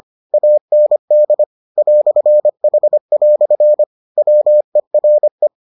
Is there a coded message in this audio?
Yes: Morse code from 0.3 to 5.5 s, heard as a keyed tone, about 25 wpm, 600 Hz. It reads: AND "H" WERE